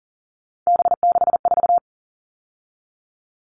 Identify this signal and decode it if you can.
Morse code, heard as a keyed tone, about 40 wpm, 700 Hz. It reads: B64
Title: All Cut Numbers - 3-Digit 40wpm